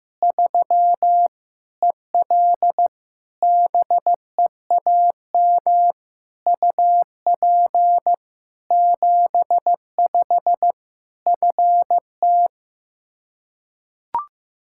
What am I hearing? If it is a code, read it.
Morse code, heard as a keyed tone, about 15 wpm, 700 Hz. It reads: 3 EL BEAM UP 75 FT
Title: K1USN SST QSO Practice 15wpm